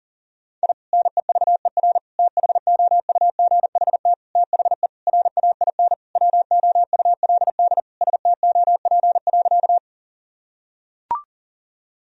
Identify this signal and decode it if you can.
Morse code, heard as a keyed tone, about 37 wpm, 700 Hz. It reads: I NEVER THOUGHT THE RAIN WOULD STOP.
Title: Sentences from Top 1000 Words - Review Part 1 - 40wpm